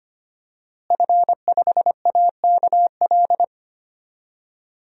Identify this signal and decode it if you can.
Morse code, heard as a keyed tone, about 25 wpm, 700 Hz. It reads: F5AKL
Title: DXCC-100 Call Signs - 5 Characters - Repeated 2x - 25wpm